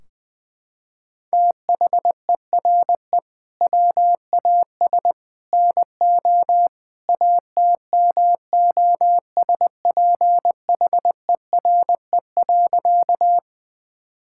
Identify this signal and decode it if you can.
Morse code, heard as a keyed tone, about 20 wpm, 700 Hz. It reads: THERE WAS NO ATMOSPHERE.
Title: Sentences from Top 1600 Words 20wpm